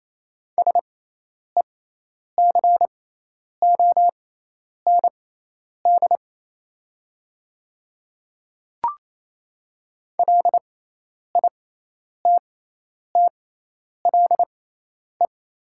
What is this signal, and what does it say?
Morse code, heard as a keyed tone, about 27 wpm, spaced as for 10 wpm, 690 Hz. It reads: SECOND LITTLE
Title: 6x Character Spacing - Top 300 Words - 28wpm